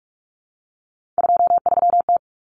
Morse code, heard as a keyed tone, about 44 wpm, 700 Hz. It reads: 23T
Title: Common Cut Numbers - 3-Digit 45wpm